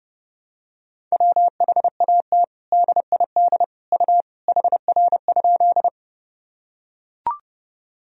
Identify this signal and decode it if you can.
Morse code, heard as a keyed tone, about 30 wpm, 700 Hz. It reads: WHAT DID U HR?
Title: Sentences from Top 300 Words and Abbreviations 30wpm